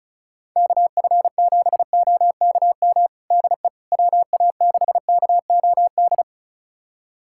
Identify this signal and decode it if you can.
Morse code, heard as a keyed tone, about 34 wpm, 700 Hz. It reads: KF7OKM DE WA6KOD
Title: USA Callsign DE Callsign - 6 Characters - 35wpm